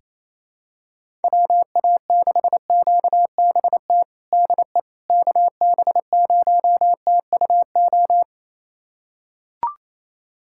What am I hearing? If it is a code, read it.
Morse code, heard as a keyed tone, about 28 wpm, 700 Hz. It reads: WA6QBT DE KB0TUO
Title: USA Callsign DE Callsign 28wpm